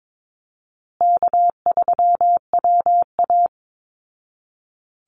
Morse code, heard as a keyed tone, about 22 wpm, 700 Hz. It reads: K3WA
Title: Easy Call Signs Repeated 3x - 22wpm